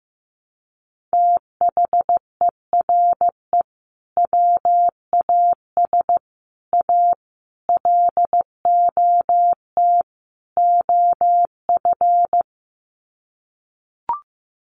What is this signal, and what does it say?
Morse code, heard as a keyed tone, about 15 wpm, 700 Hz. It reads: THERE WAS A LOT OF
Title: Sets of 5 Words 15wpm